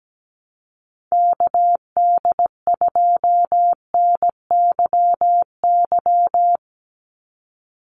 Morse code, heard as a keyed tone, about 17 wpm, 700 Hz. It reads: KD2NYY
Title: USA Call Signs 17wpm